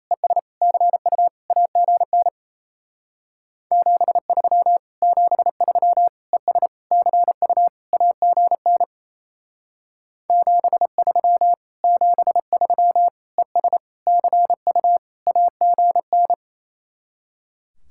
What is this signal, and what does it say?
Morse code, heard as a keyed tone, about 33 wpm, 700 Hz. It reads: ES CU AGN 73 73 ES CU AGN 73 73 ES CU AGN
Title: Speed-Racing Common QSO Phrases - 28wpm